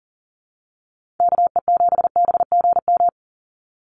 Morse code, heard as a keyed tone, about 38 wpm, 700 Hz. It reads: KE7BGM
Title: NA SOTA Activator Callsigns 40wpm